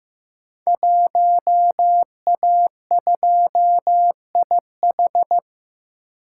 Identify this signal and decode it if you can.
Morse code, heard as a keyed tone, about 15 wpm, 700 Hz. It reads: 1A2IH
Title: Code Groups - 5 Letter:Numbers - 15wpm